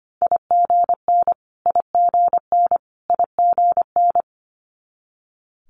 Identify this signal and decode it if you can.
Morse code, heard as a keyed tone, about 25 wpm, 700 Hz. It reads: IGN IGN IGN
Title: Multi-letter Phonemes Repeated 3x 25wpm